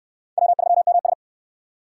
Morse code, heard as a keyed tone, about 63 wpm, 700 Hz. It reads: W3MS